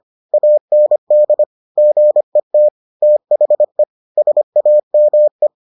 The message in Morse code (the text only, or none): AND GET THE SAME